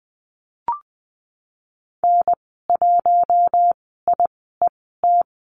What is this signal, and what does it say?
Morse code, heard as a keyed tone, about 20 wpm, spaced as for 15 wpm, 700 Hz. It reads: N1IET
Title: 062 Lesson 11g - Letter h - Call Signs 20wpm